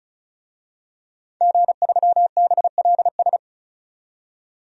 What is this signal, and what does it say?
Morse code, heard as a keyed tone, about 36 wpm, 700 Hz. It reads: G3BLS
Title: DXCC-100 Call Signs - 5 Characters - 35wpm